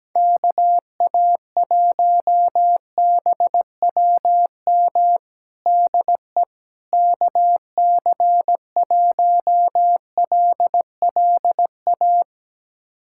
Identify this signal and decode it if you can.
Morse code, heard as a keyed tone, about 17 wpm, 700 Hz. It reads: KA1BWM DE KC1LLA
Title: USA Callsign DE Callsign 17wpm